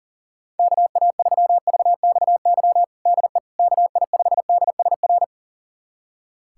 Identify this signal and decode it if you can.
Morse code, heard as a keyed tone, about 40 wpm, 700 Hz. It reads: KA3VXY DE KI5DSR
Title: USA Callsign DE Callsign 40wpm